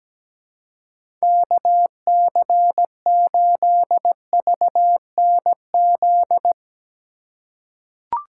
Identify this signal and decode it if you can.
Morse code, heard as a keyed tone, about 17 wpm, 700 Hz. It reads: KC8VNZ
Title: USA Call Signs - 6x Character Spacing - 17wpm